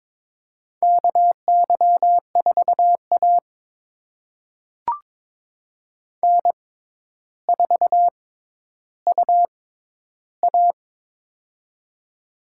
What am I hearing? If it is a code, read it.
Morse code, heard as a keyed tone, about 22 wpm, spaced as for 11 wpm, 700 Hz. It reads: KY4A N4UA
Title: USA Call Signs - 4 Characters - 6x Character Spacing - 22wpm